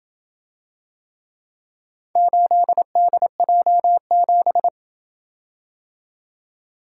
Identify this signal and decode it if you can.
Morse code, heard as a keyed tone, about 27 wpm, 700 Hz. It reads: 8DJ7